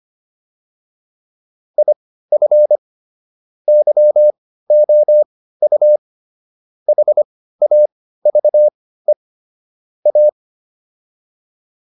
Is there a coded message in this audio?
Yes: Morse code from 1.8 to 10.3 s, heard as a keyed tone, about 25 wpm, spaced as for 15 wpm, 600 Hz. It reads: IF YOU HAVE A